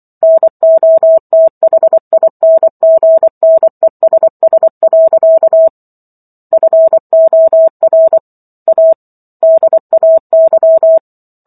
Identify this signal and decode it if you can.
Morse code, heard as a keyed tone, about 24 wpm, 650 Hz. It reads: NOTHINGNESS. FOR A DAY